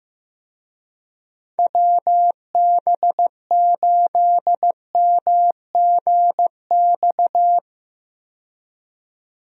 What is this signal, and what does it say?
Morse code, heard as a keyed tone, about 15 wpm, 700 Hz. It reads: WB8MGX
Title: USA Call Signs - 6 Characters - 15wpm